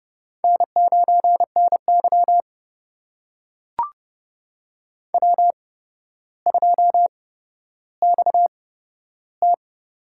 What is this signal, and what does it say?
Morse code, heard as a keyed tone, about 30 wpm, spaced as for 13 wpm, 700 Hz. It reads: N9NY W2XT
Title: USA Call Signs - 4 Characters - 8x Character Spacing - 30wpm